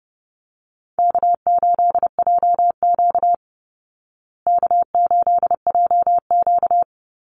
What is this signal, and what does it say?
Morse code, heard as a keyed tone, about 30 wpm, 700 Hz. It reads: K8JQ K8JQ
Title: Easy Call Signs Repeated 3x - 30wpm